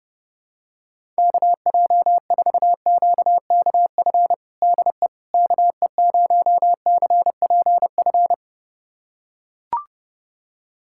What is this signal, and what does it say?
Morse code, heard as a keyed tone, about 30 wpm, 700 Hz. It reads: KJ4QKF DE KE0CPF
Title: USA Callsign DE Callsign - 6 Characters - 30wpm